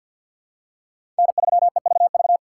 Morse code, heard as a keyed tone, about 53 wpm, 700 Hz. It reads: N3EVV
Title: USA Call Signs - 5 Characters - 3x Character Spacing - 50wpm